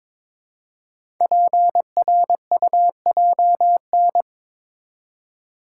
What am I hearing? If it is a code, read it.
Morse code, heard as a keyed tone, about 22 wpm, 700 Hz. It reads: PRUJN
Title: Code Groups - 5 Letters - 22wpm